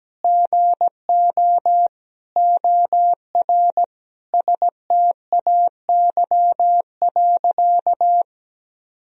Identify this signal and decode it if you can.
Morse code, heard as a keyed tone, about 17 wpm, 700 Hz. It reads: GO OR STAY.
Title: Sentences from Top 1000 Words - Review Part 3 - 17wpm